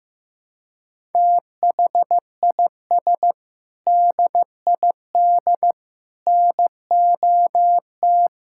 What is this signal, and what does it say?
Morse code, heard as a keyed tone, about 14 wpm, 700 Hz. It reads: THIS DID NOT
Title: Sentences from Top 1000 Words - Review Part 6 - 15wpm